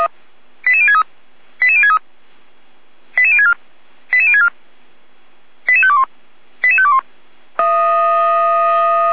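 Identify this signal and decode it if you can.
Five-tone selective call (ZVEI), heard as quick runs of five tones, tones of about 70 ms. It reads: 88063 88063 88064 88064 88521 88521
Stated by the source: ZVEI five-tone selective call received by SDR.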